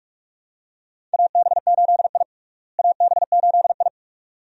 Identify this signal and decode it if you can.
Morse code, heard as a keyed tone, about 47 wpm, 700 Hz. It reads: AB8I AB8I
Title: USA Call Signs - 4 Characters - Repeated 2x - 45wpm